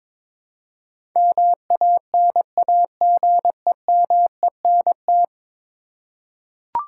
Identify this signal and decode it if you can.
Morse code, heard as a keyed tone, about 22 wpm, 700 Hz. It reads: MANAGEMENT
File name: practice / 22wpm / 10 Letter Words - First 3 Letters Given - 22wpm